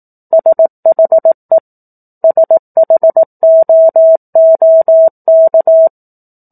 Morse code, heard as a keyed tone, about 18 wpm, 650 Hz. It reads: SHE SHOOK
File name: 04-Chapter-04 18 WPM